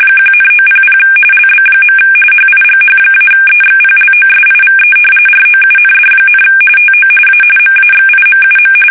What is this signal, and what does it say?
RTTY, about 45.45 baud. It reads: 1,900 LB. HIGH-EXPLOSIVE ROUND, TO A MAXIMUM
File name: fsk850 / battleshipnj